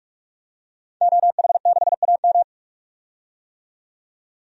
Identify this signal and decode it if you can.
Morse code, heard as a keyed tone, about 46 wpm, 700 Hz. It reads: OH6AM
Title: DXCC-100 Call Signs - 5 Characters - 45wpm